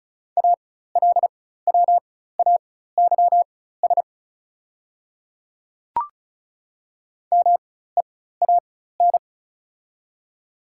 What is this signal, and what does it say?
Morse code, heard as a keyed tone, about 36 wpm, spaced as for 16 wpm, 700 Hz. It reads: ALWAYS MEAN
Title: 4x Character Spacing - Top 300 Words - 35wpm